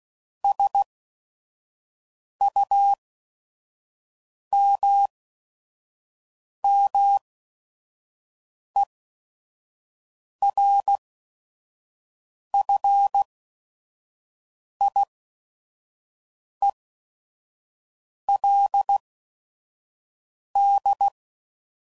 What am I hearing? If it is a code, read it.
Morse code, heard as a keyed tone, about 16 wpm, spaced as for 5 wpm, 800 Hz. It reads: SUMMERFIELD
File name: CW Code Groups @5WPM / fromSM's / Cities